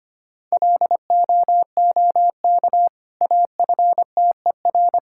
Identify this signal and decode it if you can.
Morse code, heard as a keyed tone, about 25 wpm, 700 Hz. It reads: LOOK AFTER
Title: Sentences from Top 1600 Words 25wpm